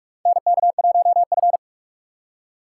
Morse code, heard as a keyed tone, about 46 wpm, 700 Hz. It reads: NK1F
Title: USA Call Signs - 4 Characters - 2x Character Spacing - 45wpm